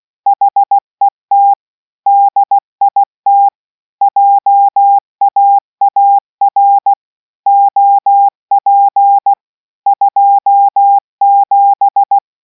Morse code, heard as a keyed tone, about 16 wpm, 800 Hz. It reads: HET DIT JAAR OP 27